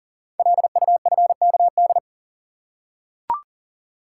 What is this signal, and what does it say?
Morse code, heard as a keyed tone, about 39 wpm, 700 Hz. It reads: LUFKD
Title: Code Groups - 5 Letters - 40wpm